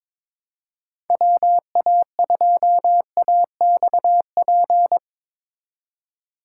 Morse code, heard as a keyed tone, about 24 wpm, 700 Hz. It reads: WA2AXP
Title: Speed Racing - USA Call Signs 22wpm